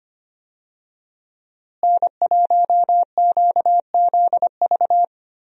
Morse code, heard as a keyed tone, about 25 wpm, 700 Hz. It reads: N1QZV
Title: USA Call Signs - 5 Characters - Repeated 2x - 25wpm